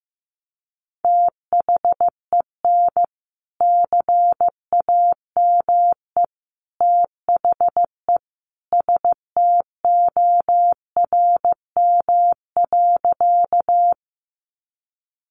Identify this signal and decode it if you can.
Morse code, heard as a keyed tone, about 15 wpm, 700 Hz. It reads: THEN CAME THE STORM.
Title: Sentences from Top 1500 Words 15wpm